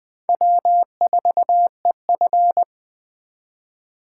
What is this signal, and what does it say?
Morse code, heard as a keyed tone, about 20 wpm, 700 Hz. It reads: W4EF